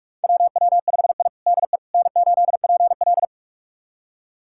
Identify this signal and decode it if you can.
Morse code, heard as a keyed tone, about 45 wpm, 700 Hz. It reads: WW5I DE N8PL
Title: USA Callsign DE Callsign - 4 Characters - 45wpm